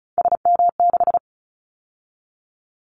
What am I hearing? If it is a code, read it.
Morse code, heard as a keyed tone, about 36 wpm, 700 Hz. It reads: SM6
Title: Callsign Prefixes for Top 25 DXCC - 35wpm